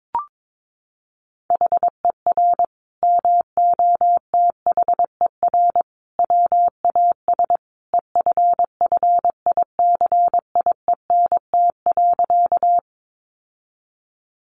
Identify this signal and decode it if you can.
Morse code, heard as a keyed tone, about 22 wpm, 700 Hz. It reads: HER MOTHER WAS EFFICIENT.
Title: Sentences from Top 1300 Words 22wpm